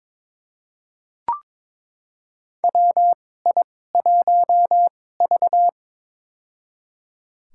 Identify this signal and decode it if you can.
Morse code, heard as a keyed tone, about 22 wpm, spaced as for 16 wpm, 700 Hz. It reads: WI1V
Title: USA Call Signs - 2x Character Spacing - 22wpm